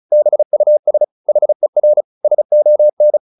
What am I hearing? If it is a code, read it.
Morse code, heard as a keyed tone, about 35 wpm, 600 Hz. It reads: BUS HER SON